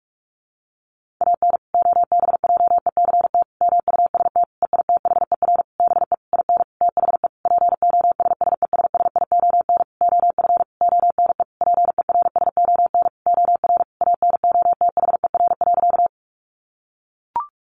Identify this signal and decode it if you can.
Morse code, heard as a keyed tone, about 44 wpm, 700 Hz. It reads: AN OBJECT MUST EITHER BE IN THE POSSESSION OF ONE PERSON OR ANOTHER.
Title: Sentences from Top 1400 Words 45wpm